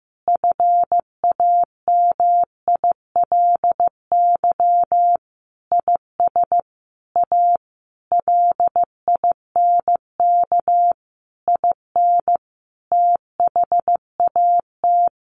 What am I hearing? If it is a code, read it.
Morse code, heard as a keyed tone, about 15 wpm, 700 Hz. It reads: FAMILY IS A LINK IN THAT
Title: Sentences from Top 1500 Words 15wpm